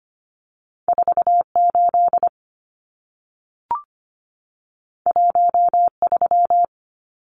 Morse code, heard as a keyed tone, about 25 wpm, 700 Hz. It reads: 48 13
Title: Numbers - 2-Digit 25wpm